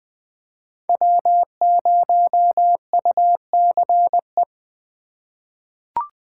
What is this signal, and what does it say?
Morse code, heard as a keyed tone, about 20 wpm, 700 Hz. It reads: W0UCE
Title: K1USN SST QSO Practice - 20wpm